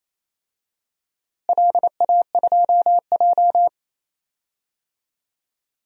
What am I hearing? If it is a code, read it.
Morse code, heard as a keyed tone, about 28 wpm, 700 Hz. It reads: LA2J
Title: DXCC-100 Call Signs - 4 Characters - Repeated 2x - 28wpm